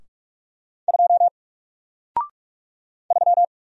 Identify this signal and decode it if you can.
Morse code, heard as a keyed tone, about 44 wpm, 700 Hz. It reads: 2 3
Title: Single Character (Letter:Number:Punctuation:Prosign) 45wpm